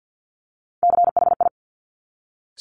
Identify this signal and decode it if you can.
Morse code, heard as a keyed tone, about 50 wpm, 700 Hz. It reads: CHI
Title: Top Three Letter Combinations in Words 50wpm